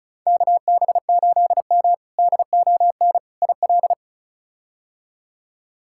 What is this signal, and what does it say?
Morse code, heard as a keyed tone, about 34 wpm, 700 Hz. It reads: KB8M DON IL
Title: K1USN SST QSO Practice - 35wpm